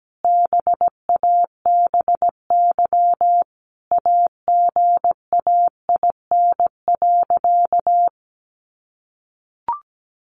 Morse code, heard as a keyed tone, about 17 wpm, 700 Hz. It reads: BABY AGAIN.
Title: Sentences from Top 1300 Words 17wpm